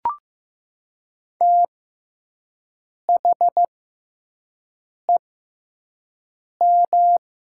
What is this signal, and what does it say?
Morse code, heard as a keyed tone, about 15 wpm, spaced as for 5 wpm, 700 Hz. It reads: THEM
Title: 6x Character Spacing - Top 100 Words - 15wpm